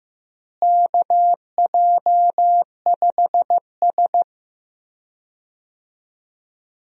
Morse code, heard as a keyed tone, about 15 wpm, 700 Hz. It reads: KJ5S
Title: USA Call Signs - 4 Characters - Repeated 2x - 15wpm